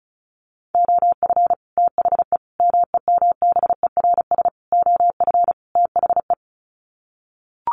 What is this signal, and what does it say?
Morse code, heard as a keyed tone, about 34 wpm, 700 Hz. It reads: OF THE MEMBERS OF THE
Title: Sets of 5 Words Encore2 35wpm